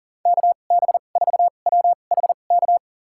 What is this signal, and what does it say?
Morse code, heard as a keyed tone, about 40 wpm, spaced as for 29 wpm, 700 Hz. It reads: KB4WHK
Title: USA Call Signs - 6 Characters - 2x Character Spacing - 40wpm